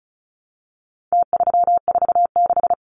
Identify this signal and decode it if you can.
Morse code, heard as a keyed tone, about 34 wpm, 700 Hz. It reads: T346
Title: Numbers - Contest Style - 4-Digit 35wpm